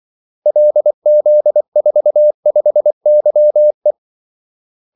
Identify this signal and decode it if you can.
Morse code, heard as a keyed tone, about 24 wpm, 600 Hz. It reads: LZ45YE
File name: Callsigns-24wpm